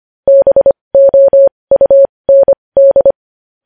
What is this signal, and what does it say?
Morse code, heard as a keyed tone, about 25 wpm, spaced as for 20 wpm, 550 Hz. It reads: BOUND
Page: news headlines in morse code at 20 wpm